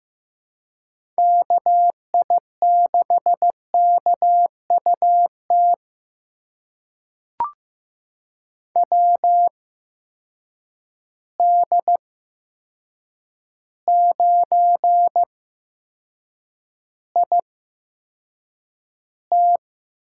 Morse code, heard as a keyed tone, about 15 wpm, spaced as for 7 wpm, 700 Hz. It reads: KI6KUT WD9IT
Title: USA Call Signs - 8x Character Spacing - 15wpm